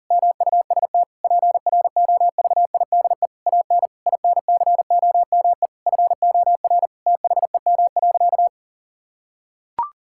Morse code, heard as a keyed tone, about 40 wpm, 700 Hz. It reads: MUST PROVIDE AN INCOME FOR THEM.
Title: Sentences from Top 1000 Words - Review Part 5 - 40wpm